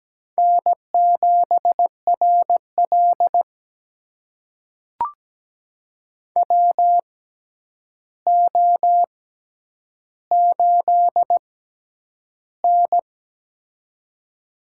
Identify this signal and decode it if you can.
Morse code, heard as a keyed tone, about 17 wpm, spaced as for 9 wpm, 700 Hz. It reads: N7RL WO8N
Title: USA Call Signs - 4 Characters - 6x Character Spacing - 17wpm